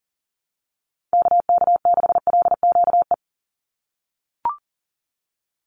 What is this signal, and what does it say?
Morse code, heard as a keyed tone, about 40 wpm, 700 Hz. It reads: KK6LQE
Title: USA Call Signs 40wpm